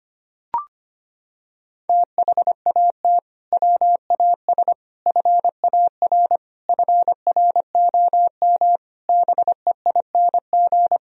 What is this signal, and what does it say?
Morse code, heard as a keyed tone, about 25 wpm, 700 Hz. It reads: THAT WAS FAR FROM BEING
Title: Sentences from Top 1000 Words - Review Part 3 - 25wpm